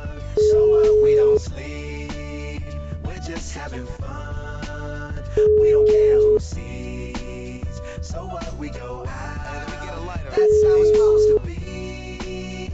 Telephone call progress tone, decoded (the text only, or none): ringback tone